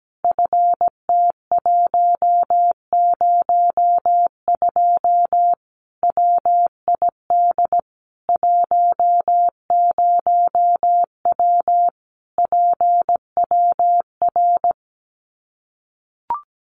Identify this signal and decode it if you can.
Morse code, heard as a keyed tone, about 17 wpm, 700 Hz. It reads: FT102 WID 10W PWR
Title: Common QSO Phrases 17wpm